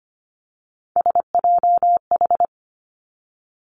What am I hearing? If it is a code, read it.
Morse code, heard as a keyed tone, about 25 wpm, 700 Hz. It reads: SJH
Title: Three Challenging Letter-Number Combinations 25wpm